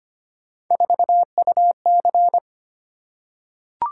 Morse code, heard as a keyed tone, about 25 wpm, 700 Hz. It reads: 4UC
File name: Three Letter-Number Combinations - 25wpm